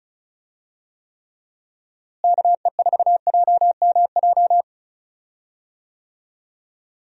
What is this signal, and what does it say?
Morse code, heard as a keyed tone, about 36 wpm, 700 Hz. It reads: KE4JMJ
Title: USA Call Signs - 6 Characters - Repeated 2x - 35wpm